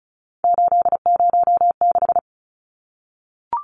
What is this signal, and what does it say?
Morse code, heard as a keyed tone, about 34 wpm, 700 Hz. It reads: 806
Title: Numbers - 3-Digit 35wpm